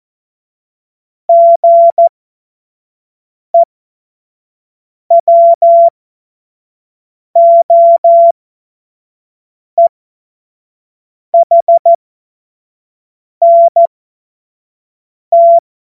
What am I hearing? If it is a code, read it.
Morse code, heard as a keyed tone, about 14 wpm, spaced as for 5 wpm, 680 Hz. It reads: GEWOEHNT
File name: Morsekurs / Witze / Witz 25. Juli 2023 25 F70